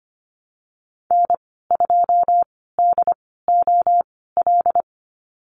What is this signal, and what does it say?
Morse code, heard as a keyed tone, about 25 wpm, spaced as for 16 wpm, 700 Hz. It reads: N2DOL
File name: Lesson 15g - Number 5 - Call Signs 25wpm